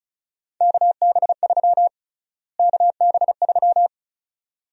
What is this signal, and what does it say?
Morse code, heard as a keyed tone, about 36 wpm, 700 Hz. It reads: KB3 KB3
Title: USA Call Sign Prefixes - Repeated 2x - 35wpm